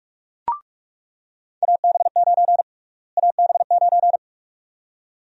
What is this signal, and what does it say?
Morse code, heard as a keyed tone, about 42 wpm, 700 Hz. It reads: AB9 AB9
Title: USA Call Sign Prefixes - Repeated 2x - 45wpm